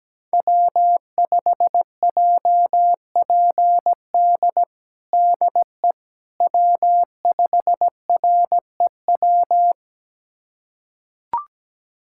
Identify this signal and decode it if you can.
Morse code, heard as a keyed tone, about 17 wpm, 700 Hz. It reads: W5JPD DE W5REW
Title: USA Callsign DE Callsign - 5 Characters - 17wpm